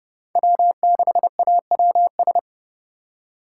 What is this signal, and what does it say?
Morse code, heard as a keyed tone, about 30 wpm, 700 Hz. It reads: W6AWS
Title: K1USN SST QSO Practice - 30wpm